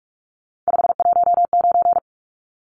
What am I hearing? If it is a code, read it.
Morse code, heard as a keyed tone, about 44 wpm, 700 Hz. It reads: 519
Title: Numbers - 3-Digit 45wpm